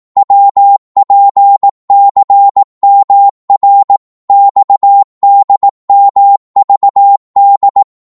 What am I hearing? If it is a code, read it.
Morse code, heard as a keyed tone, about 18 wpm, 800 Hz. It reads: WPCMR XDMVD
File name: Morse / ltr-18wpm